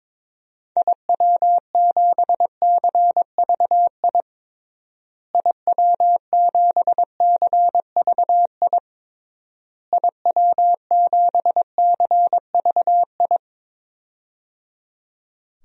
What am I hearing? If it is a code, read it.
Morse code, heard as a keyed tone, about 22 wpm, 700 Hz. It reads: IW7CVI IW7CVI IW7CVI
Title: DXCC-100 Call Signs - 6 Characters - Repeated 3x - 22wpm